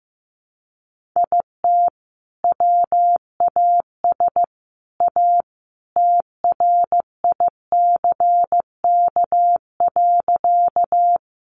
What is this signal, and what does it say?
Morse code, heard as a keyed tone, about 15 wpm, 700 Hz. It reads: IT WAS A TRICK.
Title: Sentences from Top 1500 Words 15wpm